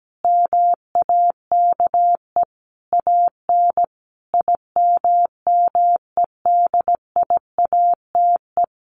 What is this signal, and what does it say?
Morse code, heard as a keyed tone, about 17 wpm, 700 Hz. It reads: MAKE AN IMMEDIATE
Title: Sentences from Top 1200 Words 17wpm